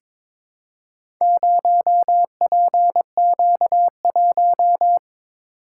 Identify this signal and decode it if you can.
Morse code, heard as a keyed tone, about 22 wpm, 700 Hz. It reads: 0PQ1
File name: practice / 22wpm / Four Letter-Number Combinations - 22wpm